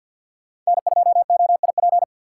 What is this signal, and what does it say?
Morse code, heard as a keyed tone, about 52 wpm, 700 Hz. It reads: N2OIP
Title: USA Call Signs - 3x Character Spacing - 50wpm